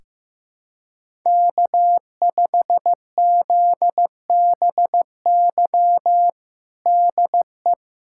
Morse code, heard as a keyed tone, about 16 wpm, 700 Hz. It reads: K5ZBY DE
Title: USA Callsign DE Callsign 15wpm